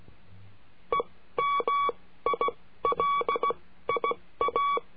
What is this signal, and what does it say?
Morse code, heard as a keyed tone, about 16 wpm, spaced as for 13 wpm, 1130 Hz. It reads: EMILIA
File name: cursos / CURSO DE TELEGRAFIA / Telegrafia - Trilha 22